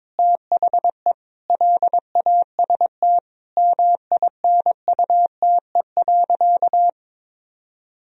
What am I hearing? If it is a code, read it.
Morse code, heard as a keyed tone, about 22 wpm, 700 Hz. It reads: THE LAST MINUTE.
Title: Sentences from Top 1000 Words - Review Part 2 - 22wpm